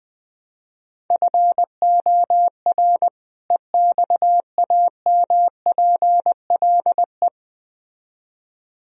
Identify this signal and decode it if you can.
Morse code, heard as a keyed tone, about 20 wpm, 700 Hz. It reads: FOR EXAMPLE
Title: Speed-Racing - Sets of 2 Words 17wpm